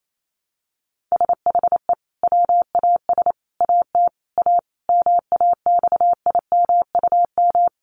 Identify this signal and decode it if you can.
Morse code, heard as a keyed tone, about 28 wpm, 700 Hz. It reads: SHE WAS AT A MAXIMUM
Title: Sentences from Top 1300 Words 28wpm